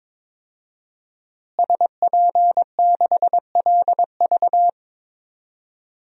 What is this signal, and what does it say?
Morse code, heard as a keyed tone, about 22 wpm, 700 Hz. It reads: SP6LV
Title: DXCC-100 Call Signs - 5 Characters - Repeated 2x - 22wpm